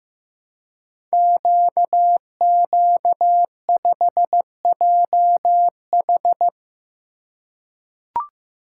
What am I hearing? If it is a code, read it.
Morse code, heard as a keyed tone, about 15 wpm, 700 Hz. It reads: QQ5JH
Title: Code Groups - 5 Letter:Numbers - 15wpm